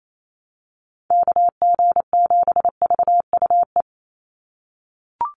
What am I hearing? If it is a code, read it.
Morse code, heard as a keyed tone, about 28 wpm, 700 Hz. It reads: KG7VUE
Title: USA Call Signs 28wpm